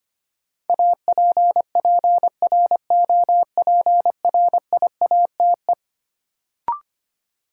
Morse code, heard as a keyed tone, about 25 wpm, 700 Hz. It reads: APPROPRIATE
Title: Sentences from Top 900 Words - New Word Drill - 25wpm